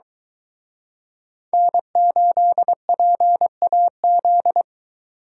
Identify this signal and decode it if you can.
Morse code, heard as a keyed tone, about 23 wpm, 700 Hz. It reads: N8PAZ